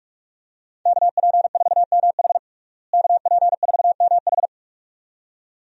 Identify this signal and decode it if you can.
Morse code, heard as a keyed tone, about 46 wpm, 700 Hz. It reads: KP4MH KP4MH
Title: DXCC-100 Call Signs - 5 Characters - Repeated 2x - 45wpm